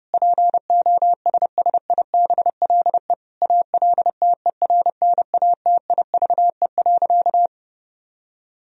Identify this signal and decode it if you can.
Morse code, heard as a keyed tone, about 30 wpm, 700 Hz. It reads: POSSIBLE ALTERNATIVE.
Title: Sentences from Top 1000 Words - Review Part 4 - 30wpm